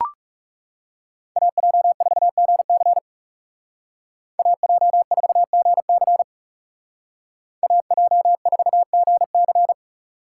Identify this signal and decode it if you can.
Morse code, heard as a keyed tone, about 40 wpm, 700 Hz. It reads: AJ4GC AJ4GC AJ4GC